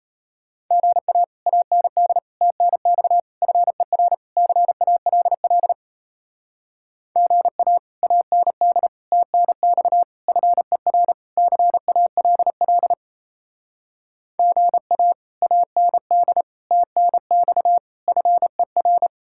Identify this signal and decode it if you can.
Morse code, heard as a keyed tone, about 36 wpm, 700 Hz. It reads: GA AND TNX FER CALL GA AND TNX FER CALL GA AND TNX FER